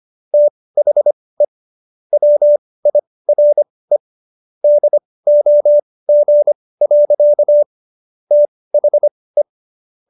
Morse code, heard as a keyed tone, about 25 wpm, spaced as for 18 wpm, 590 Hz. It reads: THE WIRE DOG. THE